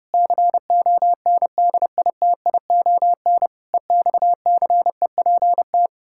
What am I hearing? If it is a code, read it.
Morse code, heard as a keyed tone, about 30 wpm, 700 Hz. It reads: CONDITION EXCEPT
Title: Pangram Challenge v2 - 30wpm